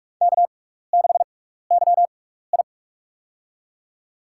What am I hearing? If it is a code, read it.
Morse code, heard as a keyed tone, about 44 wpm, spaced as for 15 wpm, 700 Hz. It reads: K6YI
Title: USA Call Signs - 4 Characters - 6x Character Spacing - 45wpm